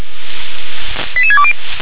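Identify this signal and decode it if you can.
Five-tone selective call (ZVEI), heard as quick runs of five tones, tones of about 70 ms. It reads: 88620
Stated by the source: ZVEI five-tone selective call received by SDR.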